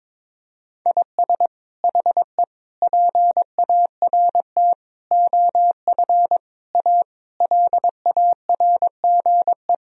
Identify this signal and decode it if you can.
Morse code, heard as a keyed tone, about 22 wpm, 700 Hz. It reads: IS HE PART OF A LARGE
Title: Sentences from Top 1000 Words - Review Part 4 - 22wpm